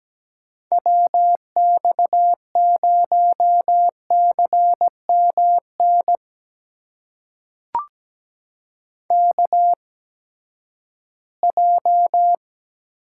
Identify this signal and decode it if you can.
Morse code, heard as a keyed tone, about 17 wpm, spaced as for 12 wpm, 700 Hz. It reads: WX0CMN KJ
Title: USA Call Signs - 6 Characters - 8x Character Spacing - 17wpm